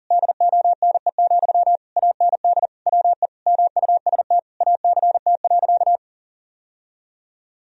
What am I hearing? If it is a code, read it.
Morse code, heard as a keyed tone, about 40 wpm, 700 Hz. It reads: DONE, AND WE MUST ACT.
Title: Sentences from Top 1000 Words - Review Part 1 - 40wpm